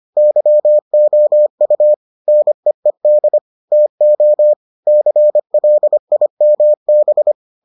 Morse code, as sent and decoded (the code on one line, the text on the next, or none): -.-- --- ..- / -. . -.. / - --- / -.-. .-.. .. -- -...
YOU NEED TO CLIMB